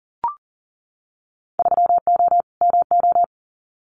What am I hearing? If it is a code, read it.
Morse code, heard as a keyed tone, about 40 wpm, 700 Hz. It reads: 3O MO
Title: Winter Field Day Exchanges Encore 40wpm